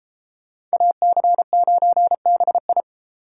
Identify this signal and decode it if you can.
Morse code, heard as a keyed tone, about 33 wpm, 700 Hz. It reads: AC9BI